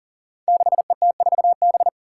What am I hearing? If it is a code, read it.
Morse code, heard as a keyed tone, about 40 wpm, 700 Hz. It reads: 6ET4B
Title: Five Challenging Letter-Number Combinations 40wpm